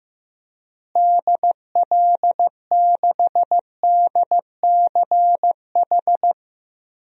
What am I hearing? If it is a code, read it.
Morse code, heard as a keyed tone, about 16 wpm, 700 Hz. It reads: DL6DCH
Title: DXCC-100 Call Signs - 6 Characters - Repeated 3x - 15wpm